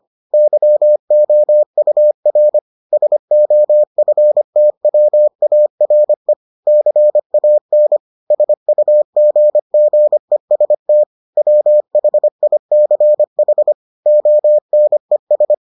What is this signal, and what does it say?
Morse code, heard as a keyed tone, about 25 wpm, 600 Hz. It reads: YOUR SOFTWARE CAN SUGGEST WHICH ONES